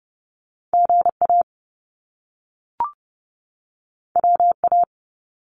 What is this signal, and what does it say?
Morse code, heard as a keyed tone, about 30 wpm, 700 Hz. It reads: GA WA
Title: US States 30wpm